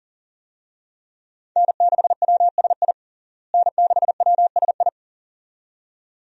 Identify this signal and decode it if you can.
Morse code, heard as a keyed tone, about 37 wpm, 700 Hz. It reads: N6WSI N6WSI
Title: USA Call Signs - 5 Characters - Repeated 2x - 40wpm